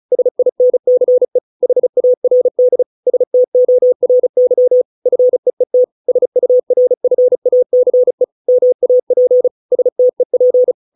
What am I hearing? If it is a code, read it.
Morse code, heard as a keyed tone, about 34 wpm, 500 Hz. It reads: SINCE HARD STORY FEET SURFACE MAP STEP